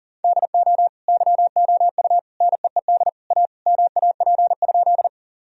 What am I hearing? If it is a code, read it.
Morse code, heard as a keyed tone, about 40 wpm, 700 Hz. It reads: DO YOU NEED A MAP?
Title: Sentences from Top 1000 Words - Review Part 2 - 40wpm